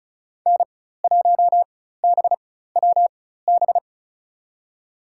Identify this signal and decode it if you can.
Morse code, heard as a keyed tone, about 36 wpm, spaced as for 16 wpm, 700 Hz. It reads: N1BWB
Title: USA Call Signs - 5 Characters - 4x Character Spacing - 35wpm